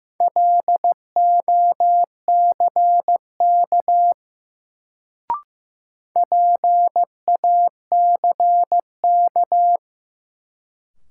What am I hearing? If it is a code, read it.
Morse code, heard as a keyed tone, about 15 wpm, 700 Hz. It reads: LOCK PACK
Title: Sentences from Top 1300 Words - New Word Drill - 15wpm